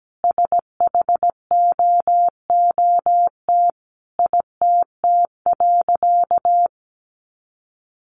Morse code, heard as a keyed tone, about 17 wpm, 700 Hz. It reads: SHOOT ITT.
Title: A Sound Of Thunder 17wpm